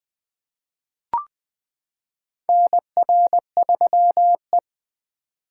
Morse code, heard as a keyed tone, about 20 wpm, 700 Hz. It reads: NR3E
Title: 2023 SOTA Chaser Call Signs - 20wpm